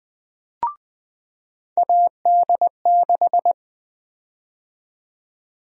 Morse code, heard as a keyed tone, about 20 wpm, 700 Hz. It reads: AD6
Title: USA Call Sign Prefixes - 20wpm